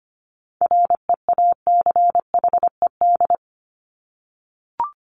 Morse code, heard as a keyed tone, about 25 wpm, 700 Hz. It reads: REACHED
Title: ED Suffix - Top 100 Words with Base Words 25wpm